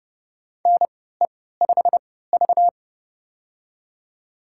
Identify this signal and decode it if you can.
Morse code, heard as a keyed tone, about 30 wpm, spaced as for 17 wpm, 700 Hz. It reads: NE5V
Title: USA Call Signs - 4 Characters - 3x Character Spacing - 30wpm